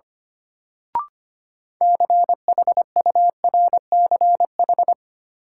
Morse code, heard as a keyed tone, about 25 wpm, 700 Hz. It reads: CHURCH
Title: Sets of 5 Words - New Word Drill - 25wpm